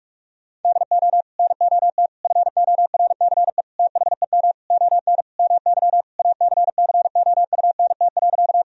Morse code, heard as a keyed tone, about 44 wpm, 700 Hz. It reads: DO NOT FORCE THEM ON MY ACCOUNT.
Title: Sentences from Top 1000 Words - Review Part 3 - 45wpm